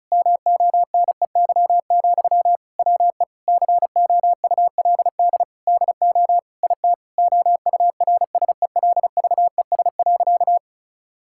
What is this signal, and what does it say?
Morse code, heard as a keyed tone, about 34 wpm, 700 Hz. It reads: MONEY, WE COULD DO IT OURSELVES.
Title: Sentences from Top 1600 Words 35wpm